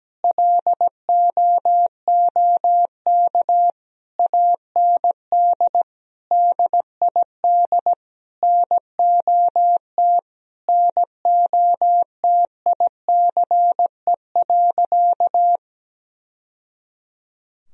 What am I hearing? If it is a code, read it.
Morse code, heard as a keyed tone, about 17 wpm, 700 Hz. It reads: LOOK AND DID NOT NOTICE.
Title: Sentences from Top 1000 Words - Review Part 2 - 17wpm